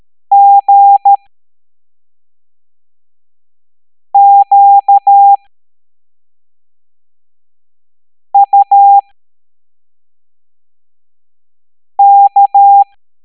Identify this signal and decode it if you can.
Morse code, heard as a keyed tone, about 13 wpm, 800 Hz. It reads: G Q U K
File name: hamdata / L07-EX03-HQP